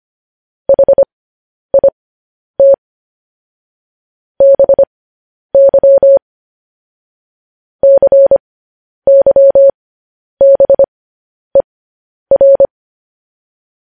Morse code, heard as a keyed tone, about 25 wpm, spaced as for 10 wpm, 550 Hz. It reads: HIT BY CYBER
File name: podcast / News headlines-1x-25x10